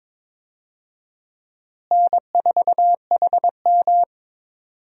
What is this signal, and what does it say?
Morse code, heard as a keyed tone, about 21 wpm, 700 Hz. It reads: N4HM
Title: USA Call Signs - 4 Characters - Repeated 3x - 22wpm